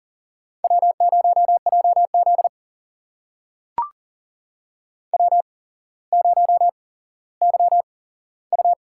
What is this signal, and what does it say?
Morse code, heard as a keyed tone, about 39 wpm, spaced as for 17 wpm, 700 Hz. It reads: W0JZ W0YU
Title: USA Call Signs - 4 Characters - 8x Character Spacing - 40wpm